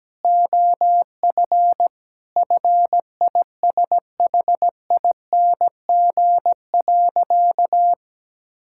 Morse code, heard as a keyed tone, about 17 wpm, 700 Hz. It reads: OF FISHING.
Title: Sentences from Top 1000 Words - Review Part 5 - 17wpm